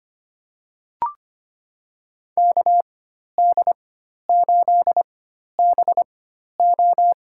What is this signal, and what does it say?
Morse code, heard as a keyed tone, about 25 wpm, spaced as for 12 wpm, 700 Hz. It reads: KD8BO
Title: USA Call Signs - 4x Character Spacing - 25wpm